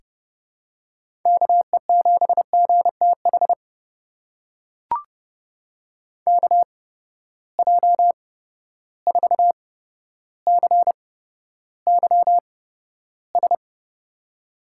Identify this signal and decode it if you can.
Morse code, heard as a keyed tone, about 31 wpm, spaced as for 13 wpm, 700 Hz. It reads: KE7GTH KJ4CYS
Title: USA Call Signs - 6 Characters - 8x Character Spacing - 30wpm